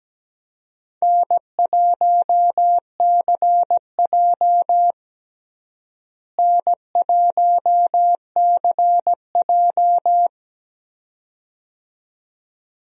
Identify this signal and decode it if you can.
Morse code, heard as a keyed tone, about 17 wpm, 700 Hz. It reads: N1CJ N1CJ
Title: USA Call Signs - 4 Characters - Repeated 2x - 17wpm